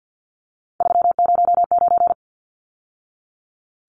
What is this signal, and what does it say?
Morse code, heard as a keyed tone, about 50 wpm, 700 Hz. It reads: 309
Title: Numbers - 3-Digit 50wpm